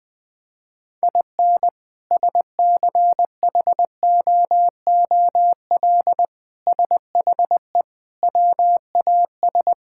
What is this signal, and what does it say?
Morse code, heard as a keyed tone, about 20 wpm, 700 Hz. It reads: IN SCHOOL SHE WAS
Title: Sentences from Top 1000 Words - Review Part 5 - 20wpm